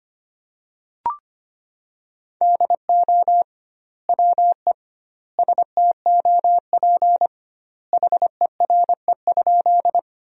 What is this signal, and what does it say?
Morse code, heard as a keyed tone, about 25 wpm, 700 Hz. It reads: DO WE STOP HERE?
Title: 2x Word-Spacing - Sentences from Top 300 Words - 25wpm